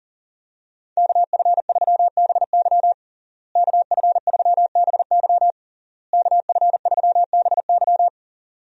Morse code, heard as a keyed tone, about 40 wpm, 700 Hz. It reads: KF3BY KF3BY KF3BY
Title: USA Call Signs - 5 Characters - Repeated 3x - 40wpm